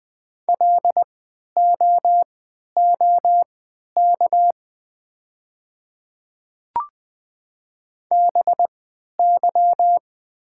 Morse code, heard as a keyed tone, about 20 wpm, spaced as for 11 wpm, 700 Hz. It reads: LOOK BY
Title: 3x Character Spacing - Top 300 Words - 20wpm